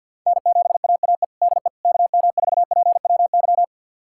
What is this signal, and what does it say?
Morse code, heard as a keyed tone, about 46 wpm, 690 Hz. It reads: N7AAE DE KM4PWY
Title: USA Callsign DE Callsign 50wpm